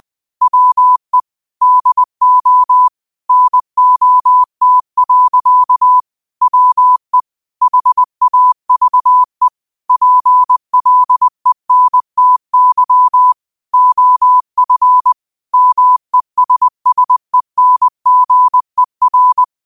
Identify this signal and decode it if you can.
Morse code, heard as a keyed tone, about 20 wpm, 1000 Hz. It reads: WE DO NOT. WE HAVE PLENTY OF MESSENGER
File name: qotdfiles / 20wpm